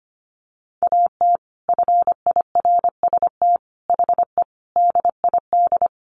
Morse code, heard as a keyed tone, about 25 wpm, 700 Hz. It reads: AT FIRST HE DID
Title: Sentences from Top 600 Words 25wpm